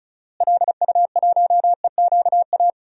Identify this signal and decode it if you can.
Morse code, heard as a keyed tone, about 36 wpm, 700 Hz. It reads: LU1EQA
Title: DXCC-100 Call Signs - 6 Characters - Repeated 3x - 35wpm